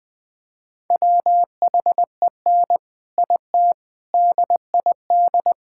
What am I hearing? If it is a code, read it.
Morse code, heard as a keyed tone, about 20 wpm, 700 Hz. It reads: WHEN IT DID
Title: Sentences from Top 1000 Words - Review Part 6 - 20wpm